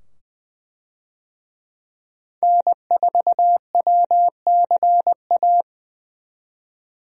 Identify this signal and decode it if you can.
Morse code, heard as a keyed tone, about 20 wpm, 700 Hz. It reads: N4WCA
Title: USA Call Signs - 5 Characters - 20wpm